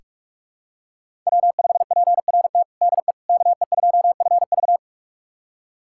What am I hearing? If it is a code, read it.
Morse code, heard as a keyed tone, about 41 wpm, 700 Hz. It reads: W5PRT DE KE2FV
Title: USA Callsign DE Callsign - 5 Characters - 45wpm